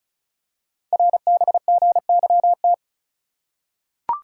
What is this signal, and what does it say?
Morse code, heard as a keyed tone, about 36 wpm, 690 Hz. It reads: RBGYT
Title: Code Groups - 5 Letters - 35wpm